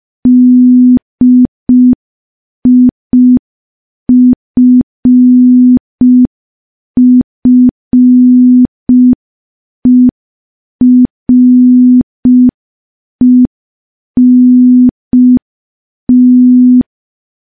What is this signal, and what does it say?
Morse code, heard as a keyed tone, about 5 wpm, 250 Hz. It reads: DIFFERENT
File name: morse / morse-training-05WPM